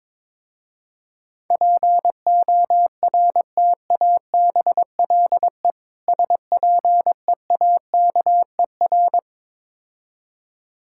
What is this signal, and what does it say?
Morse code, heard as a keyed tone, about 21 wpm, 700 Hz. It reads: PORTABLE SPEAKER
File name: Difficult Words with Hints - 22wpm